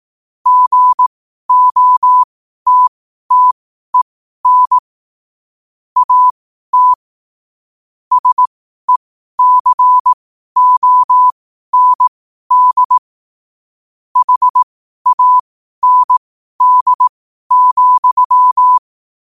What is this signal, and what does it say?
Morse code, heard as a keyed tone, about 18 wpm, spaced as for 12 wpm, 1000 Hz. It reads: GOTTEN AT SECOND HAND,
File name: 12wpm